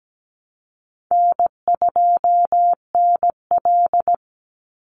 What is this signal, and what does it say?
Morse code, heard as a keyed tone, about 17 wpm, 700 Hz. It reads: N2NL
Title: Call Signs Repeated 2x - 17wpm